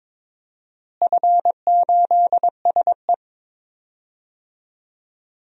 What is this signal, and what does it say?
Morse code, heard as a keyed tone, about 22 wpm, 700 Hz. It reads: F8SE
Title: DXCC-100 Call Signs - 4 Characters - Repeated 2x - 22wpm